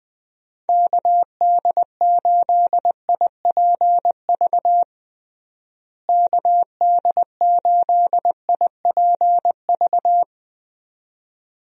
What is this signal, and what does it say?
Morse code, heard as a keyed tone, about 20 wpm, 700 Hz. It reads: KD8IPV KD8IPV
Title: USA Call Signs - 6 Characters - Repeated 2x - 20wpm